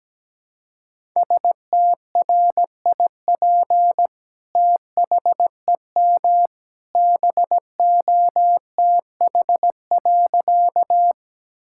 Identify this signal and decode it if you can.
Morse code, heard as a keyed tone, about 17 wpm, 700 Hz. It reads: STRIP THEM BOTH.
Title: Sentences from Top 1600 Words 17wpm